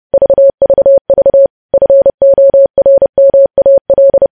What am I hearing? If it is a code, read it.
Morse code, heard as a keyed tone, about 30 wpm, 550 Hz. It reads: VVV FORMAL
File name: podcast / News headlines-1x-30x30